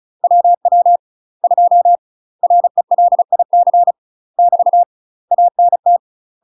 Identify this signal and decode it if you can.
Morse code, heard as a keyed tone, about 36 wpm, 700 Hz. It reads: WW 2 RELIC <BT> ANT
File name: UK MASTER QSOs TRAINING / 35WPM / UK MASTER QSOs 0007